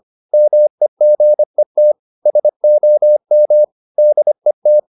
Morse code, heard as a keyed tone, about 25 wpm, 600 Hz. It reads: MEGET SOM DET
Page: Ferie 25 wpm